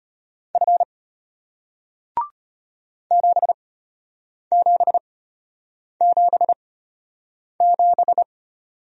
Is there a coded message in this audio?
Yes: Morse code from 0.6 to 8.2 s, heard as a keyed tone, about 34 wpm, spaced as for 8 wpm, 700 Hz. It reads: F 7777